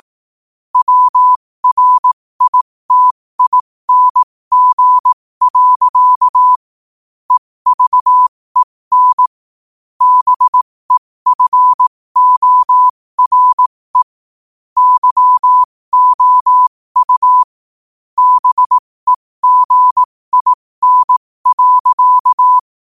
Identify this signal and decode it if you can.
Morse code, heard as a keyed tone, about 18 wpm, spaced as for 15 wpm, 1000 Hz. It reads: WRITING. EVEN BEFORE YOU BEGIN.
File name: qotdfiles / 15wpm